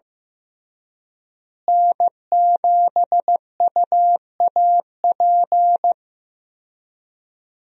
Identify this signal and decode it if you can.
Morse code, heard as a keyed tone, about 15 wpm, 700 Hz. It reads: N7UAP
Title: USA Call Signs - 5 Characters - Repeated 3x - 15wpm